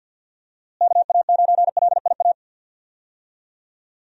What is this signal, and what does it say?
Morse code, heard as a keyed tone, about 50 wpm, 700 Hz. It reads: KA9LIA
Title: USA Call Signs - 6 Characters - Repeated 2x - 50wpm